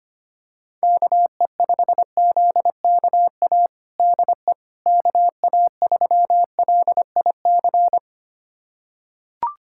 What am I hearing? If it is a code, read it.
Morse code, heard as a keyed tone, about 25 wpm, 700 Hz. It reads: KE5ZKA DE KA3LIC
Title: USA Callsign DE Callsign 25wpm